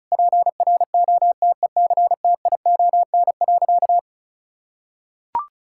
Morse code, heard as a keyed tone, about 36 wpm, 700 Hz. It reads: PROTECTION.
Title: Sentences from Top 1600 Words 35wpm